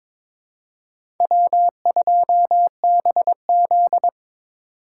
Morse code, heard as a keyed tone, about 22 wpm, 700 Hz. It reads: W2BZ
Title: USA Call Signs - 4 Characters - 4x Character Spacing - 22wpm